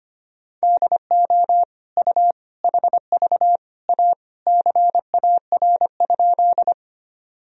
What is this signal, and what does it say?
Morse code, heard as a keyed tone, about 25 wpm, 700 Hz. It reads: DO U HV A CAR?
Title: Sentences from Top 300 Words and Abbreviations 25wpm